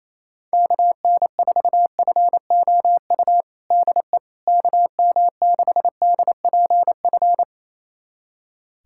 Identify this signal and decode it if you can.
Morse code, heard as a keyed tone, about 28 wpm, 700 Hz. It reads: KN4FOU DE KM6DPF
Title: USA Callsign DE Callsign - 6 Characters - 28wpm